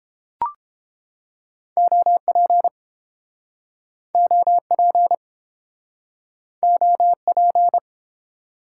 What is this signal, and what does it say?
Morse code, heard as a keyed tone, about 29 wpm, 700 Hz. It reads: OP OP OP